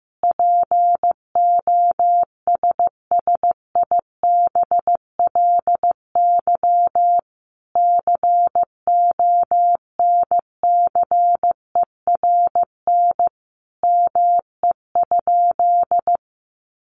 Morse code, heard as a keyed tone, about 15 wpm, 700 Hz. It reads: POSSIBLY CONCERN ME?
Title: Sentences from Top 1300 Words 15wpm